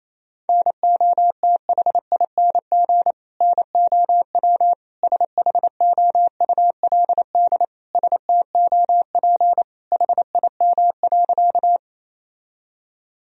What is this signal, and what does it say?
Morse code, heard as a keyed tone, about 28 wpm, 700 Hz. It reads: NOTHING NOW SHOULD STOP HIM.
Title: Sentences from Top 1000 Words - Review Part 1 - 28wpm